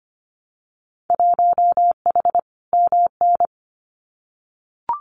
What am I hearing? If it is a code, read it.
Morse code, heard as a keyed tone, about 25 wpm, 700 Hz. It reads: 1H MN
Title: Winter Field Day Exchanges Encore 25wpm